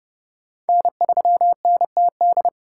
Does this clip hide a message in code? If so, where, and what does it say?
Yes: Morse code from 0.7 to 2.5 s, heard as a keyed tone, about 31 wpm, 700 Hz. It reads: N3NTD